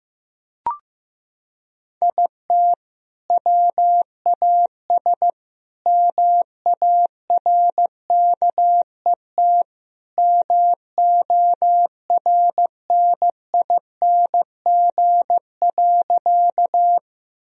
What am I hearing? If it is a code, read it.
Morse code, heard as a keyed tone, about 15 wpm, 700 Hz. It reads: IT WAS MARKET MORNING.
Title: Sentences from Top 1000 Words - Review Part 3 - 15wpm